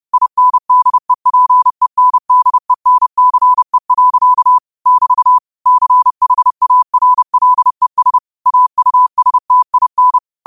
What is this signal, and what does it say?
Morse code, heard as a keyed tone, about 30 wpm, 1000 Hz. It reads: INDEPENDENCE. <BT> CHARLES AUSTIN